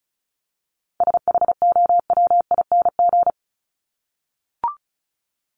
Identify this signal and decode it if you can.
Morse code, heard as a keyed tone, about 34 wpm, 700 Hz. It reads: SHOWING
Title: ING Suffix - Top 100 Words with Base Words 35wpm